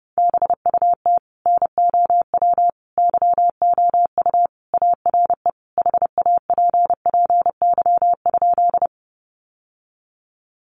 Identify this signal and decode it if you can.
Morse code, heard as a keyed tone, about 28 wpm, 700 Hz. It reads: BUT NOW YOU ARE HAPPY?
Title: Sentences from Top 1000 Words 30wpm